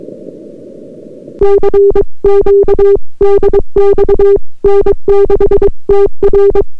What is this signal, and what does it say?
Morse code, heard as a keyed tone, about 22 wpm, 400 Hz. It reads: CQ DX N6TR